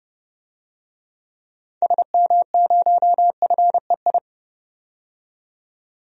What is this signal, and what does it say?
Morse code, heard as a keyed tone, about 29 wpm, 700 Hz. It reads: SM0FEI
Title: DXCC-100 Call Signs - 6 Characters - 30wpm